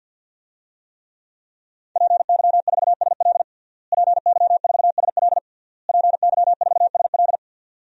Morse code, heard as a keyed tone, about 50 wpm, 700 Hz. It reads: PY4SL PY4SL PY4SL
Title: DXCC-100 Call Signs - 5 Characters - Repeated 3x - 50wpm